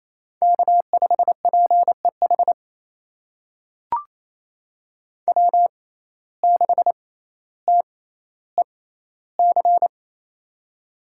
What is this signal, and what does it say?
Morse code, heard as a keyed tone, about 28 wpm, spaced as for 14 wpm, 700 Hz. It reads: K5PEH W6TEC
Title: USA Call Signs - 5 Characters - 6x Character Spacing - 28wpm